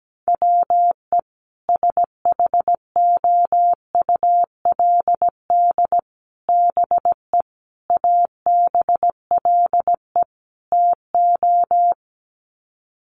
Morse code, heard as a keyed tone, about 17 wpm, 700 Hz. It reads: WE SHOULD BE ABLE TO
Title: Sets of 5 Words Encore2 17wpm